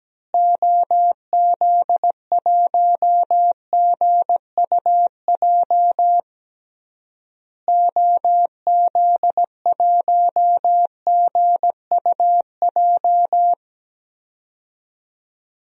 Morse code, heard as a keyed tone, about 17 wpm, 700 Hz. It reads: OZ1GUJ OZ1GUJ
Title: DXCC-100 Call Signs - 6 Characters - Repeated 3x - 17wpm